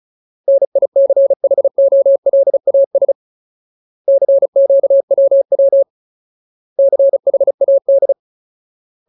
Morse code, heard as a keyed tone, about 34 wpm, 550 Hz. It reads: NICHOLAS CQWW CHAD
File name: Ultimate Word List / 35WPM Ultimate